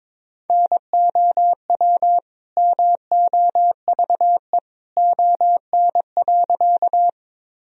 Morse code, heard as a keyed tone, about 22 wpm, 700 Hz. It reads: NOW MOVE ON.
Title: Sentences from Top 1000 Words - Review Part 1 - 22wpm